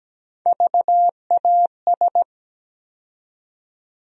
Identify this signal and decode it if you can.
Morse code, heard as a keyed tone, about 17 wpm, 700 Hz. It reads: VAS
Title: USA Call Sign Postfixes - 3 Letters 17wpm